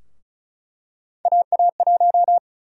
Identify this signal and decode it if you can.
Morse code, heard as a keyed tone, about 34 wpm, 700 Hz. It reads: AA1
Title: USA Call Sign Prefixes - 35wpm